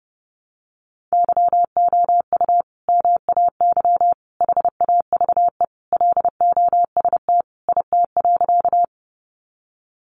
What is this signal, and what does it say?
Morse code, heard as a keyed tone, about 30 wpm, 700 Hz. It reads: YOU MAY HAVE LOST IT.
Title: Sentences from Top 1400 Words 30wpm